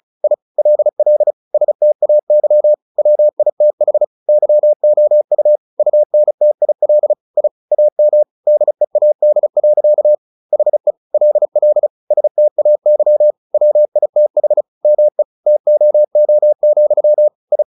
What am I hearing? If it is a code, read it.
Morse code, heard as a keyed tone, about 34 wpm, 600 Hz. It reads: I LL STAY WITH YOU UNTIL I AM DEAD. HE LL STAY WITH ME TOO, I
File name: the old man and the sea / the old man and the sea 1 35wpm